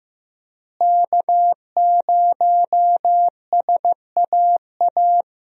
Morse code, heard as a keyed tone, about 15 wpm, 700 Hz. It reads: K0SAA
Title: USA Call Signs - 5 Characters - Repeated 3x - 15wpm